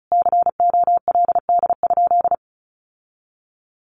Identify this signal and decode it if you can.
Morse code, heard as a keyed tone, about 34 wpm, 700 Hz. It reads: COLD?
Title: Sentences from Top 1400 Words 35wpm